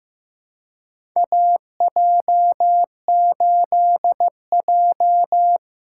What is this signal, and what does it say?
Morse code, heard as a keyed tone, about 15 wpm, 700 Hz. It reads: AJ8J
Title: USA Call Signs - 4 Characters - 4x Character Spacing - 15wpm